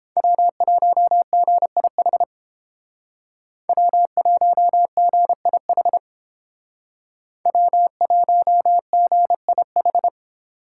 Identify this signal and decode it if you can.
Morse code, heard as a keyed tone, about 28 wpm, 700 Hz. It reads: W1GIH W1GIH W1GIH